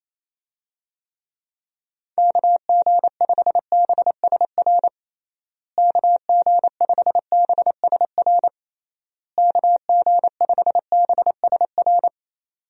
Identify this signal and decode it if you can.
Morse code, heard as a keyed tone, about 27 wpm, 700 Hz. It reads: KG5BSR KG5BSR KG5BSR
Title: USA Call Signs - 6 Characters - Repeated 3x - 28wpm